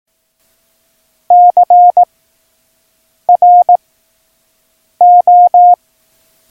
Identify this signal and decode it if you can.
Morse code, heard as a keyed tone, about 18 wpm, 710 Hz. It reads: C R O
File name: acs / Instrument Rating / Instrument ACS